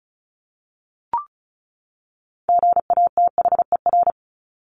Morse code, heard as a keyed tone, about 34 wpm, 700 Hz. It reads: GATHER
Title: Top 1501-1750 Words 35wpm